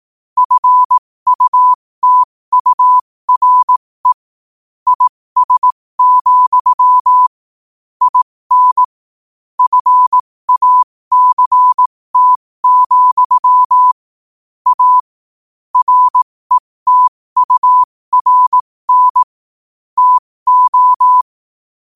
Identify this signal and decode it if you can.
Morse code, heard as a keyed tone, about 18 wpm, spaced as for 15 wpm, 1000 Hz. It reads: FUTURE IS, IN FACT, A RETURN TO
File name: qotdfiles / 15wpm